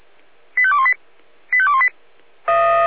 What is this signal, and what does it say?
Five-tone selective call (ZVEI), heard as quick runs of five tones, tones of about 70 ms. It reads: 85218 85218